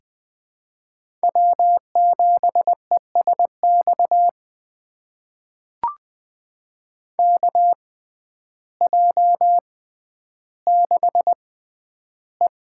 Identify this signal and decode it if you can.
Morse code, heard as a keyed tone, about 20 wpm, spaced as for 11 wpm, 700 Hz. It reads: W7ESX KJ6E
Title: USA Call Signs - 6x Character Spacing - 20wpm